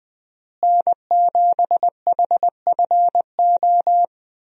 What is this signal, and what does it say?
Morse code, heard as a keyed tone, about 20 wpm, 700 Hz. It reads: N7HFO
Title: USA Call Signs - 5 Characters - Repeated 3x - 20wpm